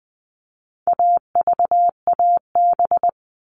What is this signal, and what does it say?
Morse code, heard as a keyed tone, about 20 wpm, 700 Hz. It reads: AVAB
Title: All Cut Numbers - 4-Digit 20wpm